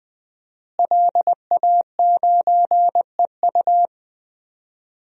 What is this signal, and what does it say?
Morse code, heard as a keyed tone, about 21 wpm, 700 Hz. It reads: LA9EU
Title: DXCC-100 Call Signs - 5 Characters - Repeated 2x - 20wpm